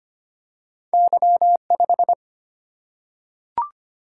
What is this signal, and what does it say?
Morse code, heard as a keyed tone, about 25 wpm, 700 Hz. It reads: Y 5
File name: Two Letter-Number Combinations - 25wpm